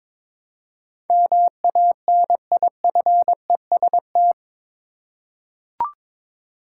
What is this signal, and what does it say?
Morse code, heard as a keyed tone, about 22 wpm, 700 Hz. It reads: MANIFEST
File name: practice / 22wpm / Eight Letter Words - 22wpm